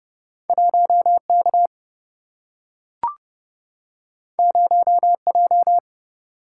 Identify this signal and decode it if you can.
Morse code, heard as a keyed tone, about 30 wpm, 700 Hz. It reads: 1K 0J
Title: Single-Letter Single-Number 30wpm